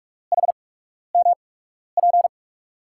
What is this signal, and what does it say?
Morse code, heard as a keyed tone, about 44 wpm, spaced as for 12 wpm, 700 Hz. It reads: HMP
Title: USA Call Signs - 8x Character Spacing - 45wpm